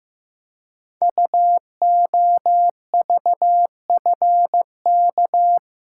Morse code, heard as a keyed tone, about 15 wpm, 700 Hz. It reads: UOVFK
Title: Code Groups - 5 Letters - 15wpm